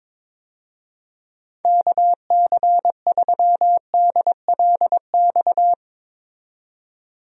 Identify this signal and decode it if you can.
Morse code, heard as a keyed tone, about 22 wpm, 700 Hz. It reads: KC3DLX
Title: USA Call Signs - 6 Characters - 22wpm